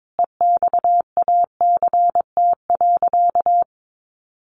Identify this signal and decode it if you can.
Morse code, heard as a keyed tone, about 22 wpm, 700 Hz. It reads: EXACT.
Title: Sentences from Top 1300 Words 22wpm